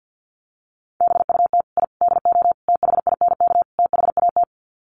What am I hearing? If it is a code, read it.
Morse code, heard as a keyed tone, about 50 wpm, 700 Hz. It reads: BUT I DO THINK THAT
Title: Sets of 5 Words Encore2 50wpm